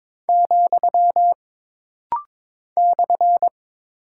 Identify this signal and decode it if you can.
Morse code, heard as a keyed tone, about 22 wpm, 700 Hz. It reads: , /
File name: practice / 22wpm / Single Punctuation-Prosign - Rapid-Fire - 22wpm